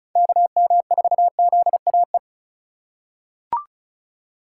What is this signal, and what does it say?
Morse code, heard as a keyed tone, about 34 wpm, 700 Hz. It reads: KM4ZAE
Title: USA Call Signs - 6 Characters - 4x Character Spacing - 35wpm